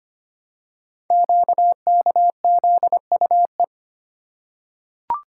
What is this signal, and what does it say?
Morse code, heard as a keyed tone, about 25 wpm, 700 Hz. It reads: QKZUE
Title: Code Groups - 5 Letters - 25wpm